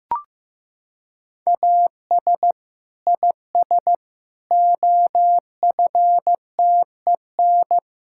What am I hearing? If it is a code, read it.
Morse code, heard as a keyed tone, about 15 wpm, 700 Hz. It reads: AS IS OFTEN
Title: Sentences from Top 1000 Words - Review Part 5 - 15wpm